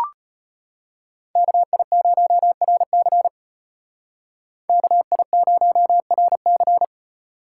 Morse code, heard as a keyed tone, about 33 wpm, 700 Hz. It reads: KI0RC KI0RC